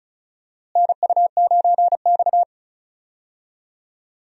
Morse code, heard as a keyed tone, about 34 wpm, 700 Hz. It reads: NU9X
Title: USA Call Signs - 4 Characters - Repeated 3x - 35wpm